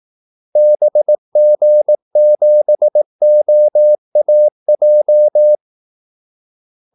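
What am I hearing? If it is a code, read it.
Morse code, heard as a keyed tone, about 18 wpm, 600 Hz. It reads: BG7OAJ
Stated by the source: Morse call-sign practice at 18 wpm.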